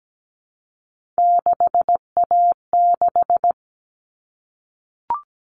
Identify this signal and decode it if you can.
Morse code, heard as a keyed tone, about 17 wpm, 700 Hz. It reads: 6A6
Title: All Cut Numbers - 3-Digit 17wpm